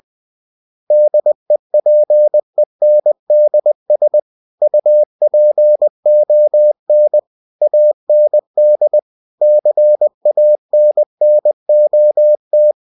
Morse code, heard as a keyed tone, about 20 wpm, 600 Hz. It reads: DEPENDS UPON AND CANNOT